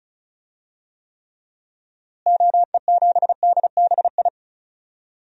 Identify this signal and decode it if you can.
Morse code, heard as a keyed tone, about 36 wpm, 700 Hz. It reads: OE7DBI
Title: DXCC-100 Call Signs - 6 Characters - Repeated 3x - 35wpm